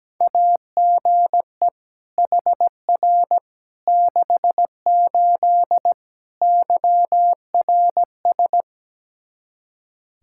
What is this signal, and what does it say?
Morse code, heard as a keyed tone, about 17 wpm, 700 Hz. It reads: AGE HR 68 YRS
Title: K1USN SST QSO Practice 17wpm